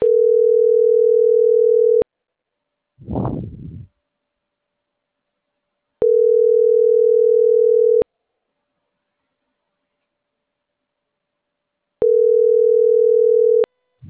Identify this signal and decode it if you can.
Ringback tone (North America), 440 + 480 Hz.
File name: CENTRAL MAYAB / out dulce.silva